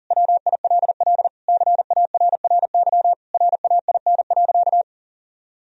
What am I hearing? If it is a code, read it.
Morse code, heard as a keyed tone, about 40 wpm, 700 Hz. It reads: WILL CARRY RAIN.
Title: Sentences from Top 1000 Words - Review Part 3 - 40wpm